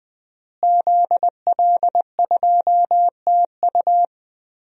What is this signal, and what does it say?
Morse code, heard as a keyed tone, about 20 wpm, 700 Hz. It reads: ZL2TU
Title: DXCC-100 Call Signs - 5 Characters - Repeated 3x - 20wpm